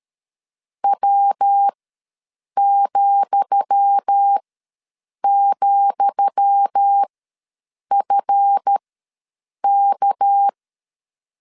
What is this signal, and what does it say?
Morse code, heard as a keyed tone, about 13 wpm, spaced as for 7 wpm, 800 Hz. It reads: W,,FK
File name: LEZIONI / 10 SIGNS / 10D COMA ws 12wpm 3618